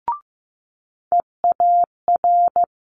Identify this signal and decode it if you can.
Morse code, heard as a keyed tone, about 15 wpm, 700 Hz. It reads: EAR
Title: Multi-letter Phonemes Quick Fire 15wpm